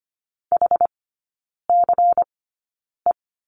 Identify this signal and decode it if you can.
Morse code, heard as a keyed tone, about 25 wpm, spaced as for 9 wpm, 700 Hz. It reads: HCE
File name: course / 25:10wpm / 093 Lesson 16e - Letter c - Three Characters 25wpm